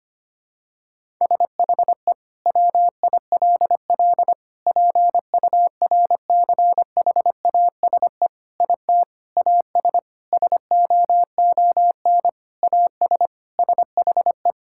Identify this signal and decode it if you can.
Morse code, heard as a keyed tone, about 25 wpm, 700 Hz. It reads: SHE WILL PURCHASE IT AS SOON AS SHE
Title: Sentences from Top 1000 Words - Review Part 5 - 25wpm